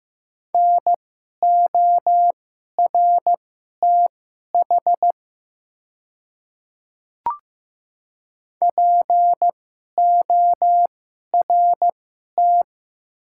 Morse code, heard as a keyed tone, about 15 wpm, spaced as for 11 wpm, 700 Hz. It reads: NORTH PORT
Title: 2x Character Spacing - Top 300 Words - 15wpm